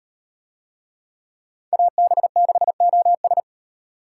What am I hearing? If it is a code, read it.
Morse code, heard as a keyed tone, about 38 wpm, 700 Hz. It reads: AB6OS